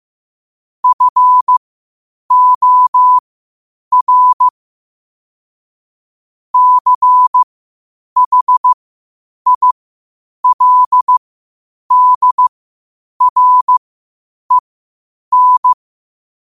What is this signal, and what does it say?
Morse code, heard as a keyed tone, about 15 wpm, spaced as for 8 wpm, 1000 Hz. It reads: FOR CHILDREN
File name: qotdfiles / 8wpm